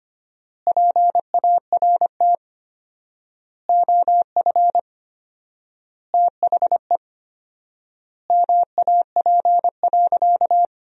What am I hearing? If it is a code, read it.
Morse code, heard as a keyed tone, about 25 wpm, 700 Hz. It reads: PART OF THE MAP.
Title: 4x Word-Spacing - Sentences from Top 400 Words - 25wpm